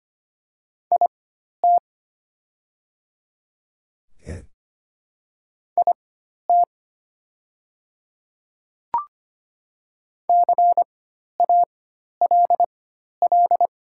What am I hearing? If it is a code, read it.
Morse code, heard as a keyed tone, about 25 wpm, spaced as for 12 wpm, 700 Hz. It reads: IT IT CALL
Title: 4x Character Spacing - Top 100 Words - 25wpm